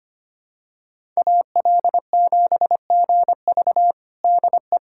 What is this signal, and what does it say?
Morse code, heard as a keyed tone, about 25 wpm, 700 Hz. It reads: AL7GV DE
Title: USA Callsign DE Callsign - 5 Characters - 25wpm